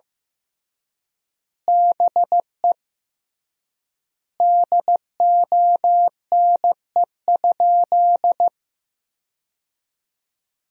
Morse code, heard as a keyed tone, about 15 wpm, 700 Hz. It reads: BE DONE?
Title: 3x Word-Spacing - Sentences from Top 500 Words - 15wpm